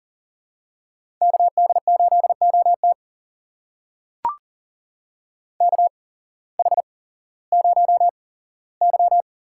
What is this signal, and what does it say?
Morse code, heard as a keyed tone, about 40 wpm, spaced as for 19 wpm, 700 Hz. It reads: KD8OT KH0Y
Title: USA Call Signs - 5 Characters - 8x Character Spacing - 40wpm